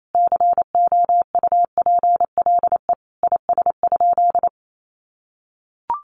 Morse code, heard as a keyed tone, about 28 wpm, 700 Hz. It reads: COUPLE IS?
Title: Sentences from Top 1100 Words 28wpm